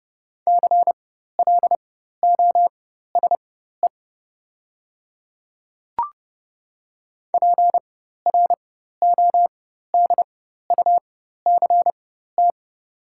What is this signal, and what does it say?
Morse code, heard as a keyed tone, about 30 wpm, spaced as for 14 wpm, 700 Hz. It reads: CLOSE PRODUCT
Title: 4x Character Spacing - Top 500 Words - 30wpm